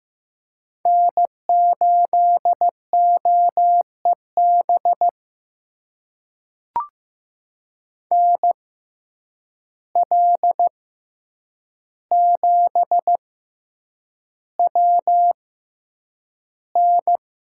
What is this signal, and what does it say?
Morse code, heard as a keyed tone, about 15 wpm, spaced as for 8 wpm, 700 Hz. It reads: N8OEB NL7WN
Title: USA Call Signs - 5 Characters - 6x Character Spacing - 15wpm